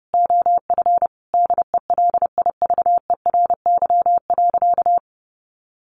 Morse code, heard as a keyed tone, about 28 wpm, 700 Hz. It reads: OF DELIVERY.
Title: Sentences from Top 1500 Words 30wpm